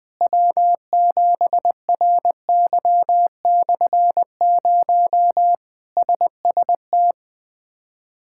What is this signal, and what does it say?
Morse code, heard as a keyed tone, about 21 wpm, 700 Hz. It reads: W7RY/0 SST
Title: K1USN SST QSO Practice - 20wpm